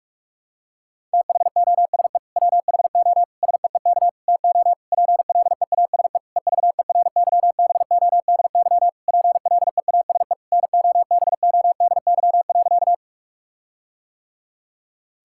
Morse code, heard as a keyed tone, about 46 wpm, 700 Hz. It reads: THOSE WHO SEEK TO PLEASE EVERYBODY PLEASE NOBODY.
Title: Sentences from Top 1600 Words 45wpm